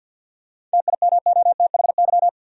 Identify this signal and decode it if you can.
Morse code, heard as a keyed tone, about 52 wpm, 700 Hz. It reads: TIMOTHY
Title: USA Common Names - 50wpm